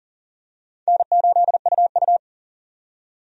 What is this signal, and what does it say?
Morse code, heard as a keyed tone, about 40 wpm, 700 Hz. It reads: N8UU
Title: USA Call Signs - 4 Characters - 40wpm